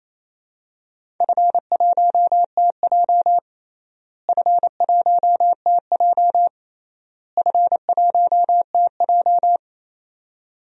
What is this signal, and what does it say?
Morse code, heard as a keyed tone, about 28 wpm, 700 Hz. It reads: F1TJ F1TJ F1TJ
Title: DXCC-100 Call Signs - 4 Characters - Repeated 3x - 28wpm